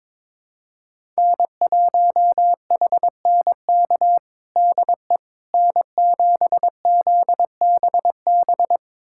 Morse code, heard as a keyed tone, about 22 wpm, 700 Hz. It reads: N1HNK DE N7ZBB
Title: USA Callsign DE Callsign - 5 Characters - 22wpm